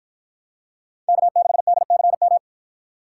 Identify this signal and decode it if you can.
Morse code, heard as a keyed tone, about 49 wpm, 700 Hz. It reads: K6DCM